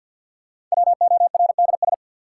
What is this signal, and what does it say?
Morse code, heard as a keyed tone, about 53 wpm, 700 Hz. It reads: WORDS
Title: Sets of 3 Words - New Word Drill - 50wpm